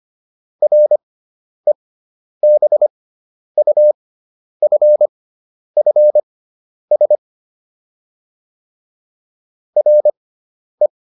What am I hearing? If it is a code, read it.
Morse code, heard as a keyed tone, about 25 wpm, spaced as for 10 wpm, 600 Hz. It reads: REBUFFS RE